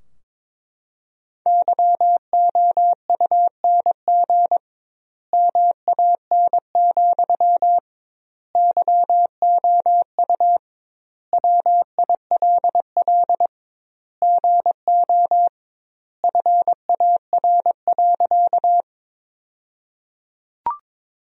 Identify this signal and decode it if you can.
Morse code, heard as a keyed tone, about 22 wpm, 700 Hz. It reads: YOUNG MAN, YOU WILL GO FAR.
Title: 2x Word-Spacing - Sentences from Top 400 Words - 22wpm